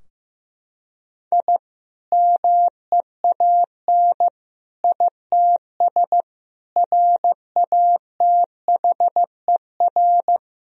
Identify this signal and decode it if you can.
Morse code, heard as a keyed tone, about 16 wpm, 700 Hz. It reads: I MEAN ITS RATHER
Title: Sentences from Top 1000 Words - Review Part 4 - 15wpm